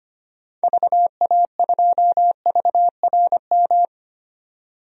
Morse code, heard as a keyed tone, about 25 wpm, 700 Hz. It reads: VA2VRM
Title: DXCC-100 Call Signs - 6 Characters - 25wpm